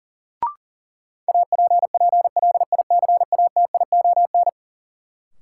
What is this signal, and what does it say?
Morse code, heard as a keyed tone, about 40 wpm, 700 Hz. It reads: APPLICATION
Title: Sentences from Top 1200 Words - New Word Drill Alt - 40wpm